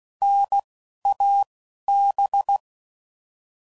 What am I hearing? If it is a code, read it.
Morse code, heard as a keyed tone, about 16 wpm, 800 Hz. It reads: N A B
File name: Prefixes / By First Letter / Ns